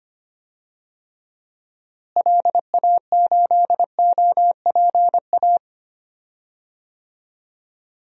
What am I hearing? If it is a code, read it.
Morse code, heard as a keyed tone, about 25 wpm, 700 Hz. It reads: LA8OPA
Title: DXCC-100 Call Signs - 6 Characters - Repeated 2x - 25wpm